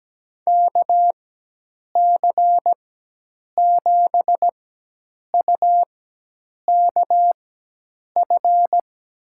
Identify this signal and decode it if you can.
Morse code, heard as a keyed tone, about 17 wpm, spaced as for 8 wpm, 700 Hz. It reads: KC7UKF
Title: USA Call Signs - 4x Character Spacing - 17wpm